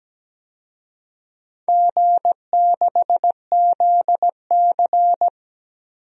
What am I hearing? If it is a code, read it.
Morse code, heard as a keyed tone, about 17 wpm, 700 Hz. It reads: G6ZC
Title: DXCC-100 Call Signs - 4 Characters - Repeated 2x - 17wpm